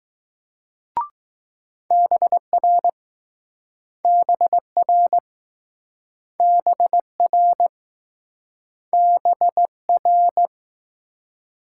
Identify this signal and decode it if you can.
Morse code, heard as a keyed tone, about 21 wpm, 700 Hz. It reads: BR BR BR BR